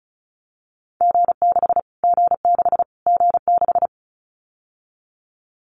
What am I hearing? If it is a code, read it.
Morse code, heard as a keyed tone, about 34 wpm, 700 Hz. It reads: G6 G6 G6
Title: Callsign Prefixes for Top 25 DXCC - Repeated 3x - 35wpm